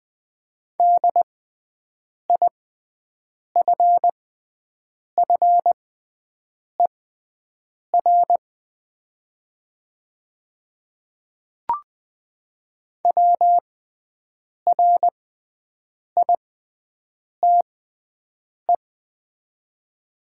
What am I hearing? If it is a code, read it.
Morse code, heard as a keyed tone, about 20 wpm, spaced as for 7 wpm, 700 Hz. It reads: DIFFER WRITE
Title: 6x Character Spacing - Top 200 Words - 20wpm